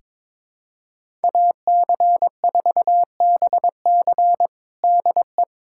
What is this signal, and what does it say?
Morse code, heard as a keyed tone, about 22 wpm, 700 Hz. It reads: AC4BC DE
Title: USA Callsign DE Callsign - 5 Characters - 22wpm